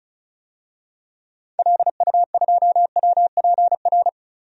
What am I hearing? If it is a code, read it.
Morse code, heard as a keyed tone, about 36 wpm, 700 Hz. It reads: LU2WPR
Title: DXCC-100 Call Signs - 6 Characters - Repeated 2x - 35wpm